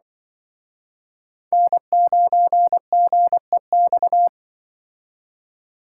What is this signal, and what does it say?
Morse code, heard as a keyed tone, about 25 wpm, 700 Hz. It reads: N9GEX
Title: Speed Racing - USA 1x3 Call Signs 20wpm